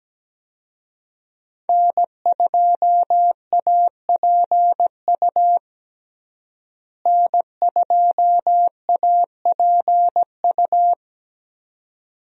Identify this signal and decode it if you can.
Morse code, heard as a keyed tone, about 17 wpm, 700 Hz. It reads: N2APU N2APU
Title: USA Call Signs - 5 Characters - Repeated 2x - 17wpm